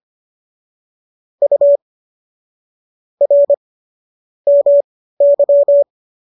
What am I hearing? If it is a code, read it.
Morse code, heard as a keyed tone, about 25 wpm, spaced as for 15 wpm, 570 Hz. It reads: U R MY